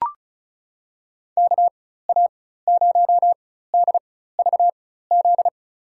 Morse code, heard as a keyed tone, about 34 wpm, spaced as for 16 wpm, 700 Hz. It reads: KA0DVZ
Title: USA Call Signs - 6 Characters - 4x Character Spacing - 35wpm